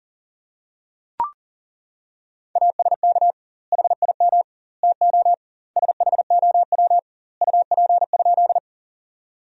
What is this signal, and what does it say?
Morse code, heard as a keyed tone, about 40 wpm, 700 Hz. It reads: ASK HIM TO SHOW UP?
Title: 2x Word-Spacing - Sentences from Top 200 Words - 40wpm